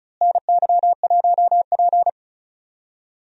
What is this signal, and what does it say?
Morse code, heard as a keyed tone, about 36 wpm, 700 Hz. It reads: NY1P
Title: USA Call Signs - 4 Characters - Repeated 2x - 35wpm